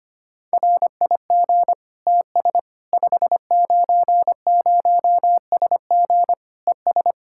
Morse code, heard as a keyed tone, about 25 wpm, 700 Hz. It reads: RIG TS 590SG ES
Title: Practice QSO - Exchange 2 - Info - 25wpm